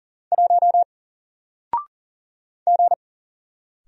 Morse code, heard as a keyed tone, about 40 wpm, 700 Hz. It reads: 1 G
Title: Single Letter-Number 40wpm